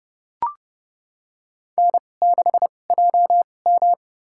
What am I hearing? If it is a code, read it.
Morse code, heard as a keyed tone, about 31 wpm, spaced as for 22 wpm, 700 Hz. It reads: N6JM
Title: USA Call Signs - 4 Characters - 2x Character Spacing - 30wpm